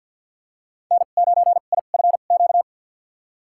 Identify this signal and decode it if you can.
Morse code, heard as a keyed tone, about 52 wpm, spaced as for 36 wpm, 700 Hz. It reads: N9IFQ
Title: USA Call Signs - 2x Character Spacing - 50wpm